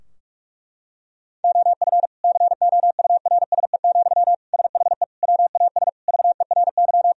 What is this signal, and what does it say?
Morse code, heard as a keyed tone, about 46 wpm, 700 Hz. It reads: OF COURSE, SHE WAS VERY
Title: Sentences from Top 1000 Words - Review Part 4 - 45wpm